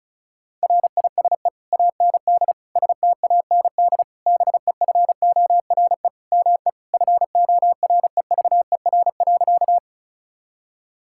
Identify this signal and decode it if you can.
Morse code, heard as a keyed tone, about 34 wpm, 700 Hz. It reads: RISE AND STAND BEFORE ME FOREVER.
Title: Sentences from Top 1600 Words 35wpm